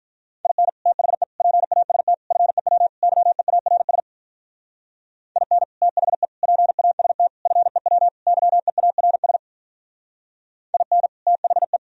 Morse code, heard as a keyed tone, about 42 wpm, 700 Hz. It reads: IN THE PAST FEW YEARS IN THE PAST FEW YEARS IN THE
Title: Speed-Racing - Sets of 5 Words 35wpm